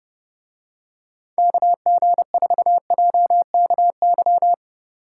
Morse code, heard as a keyed tone, about 30 wpm, 700 Hz. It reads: KG4JKY